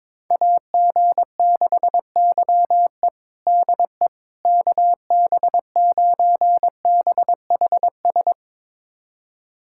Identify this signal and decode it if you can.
Morse code, heard as a keyed tone, about 22 wpm, 700 Hz. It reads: AG6YE DE KB9BHS
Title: USA Callsign DE Callsign 22wpm